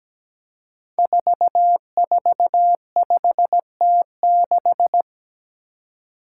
Morse code, heard as a keyed tone, about 17 wpm, 700 Hz. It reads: 445T6
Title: Five Challenging Letter-Number Combinations 17wpm